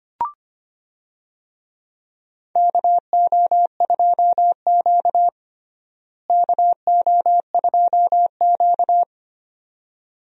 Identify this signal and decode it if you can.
Morse code, heard as a keyed tone, about 25 wpm, 700 Hz. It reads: KO2Q KO2Q
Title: USA Call Signs - 4 Characters - Repeated 2x - 25wpm